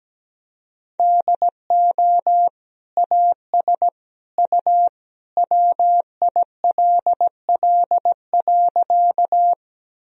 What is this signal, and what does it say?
Morse code, heard as a keyed tone, about 17 wpm, 700 Hz. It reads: DO AS U WILL.
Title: Sentences from Top 100 Words and Abbreviations 17wpm